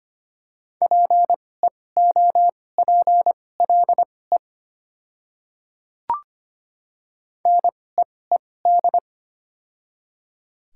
Morse code, heard as a keyed tone, about 25 wpm, spaced as for 18 wpm, 700 Hz. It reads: PEOPLE NEED
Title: 2x Character Spacing - Top 300 Words - 25wpm